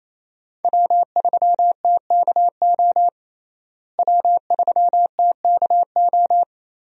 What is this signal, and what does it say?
Morse code, heard as a keyed tone, about 27 wpm, 700 Hz. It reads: W3TKO W3TKO
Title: USA Call Signs - 5 Characters - Repeated 3x - 28wpm